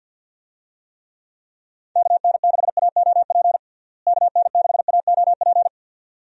Morse code, heard as a keyed tone, about 51 wpm, 700 Hz. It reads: KN6AOP KN6AOP
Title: USA Call Signs - 6 Characters - Repeated 2x - 50wpm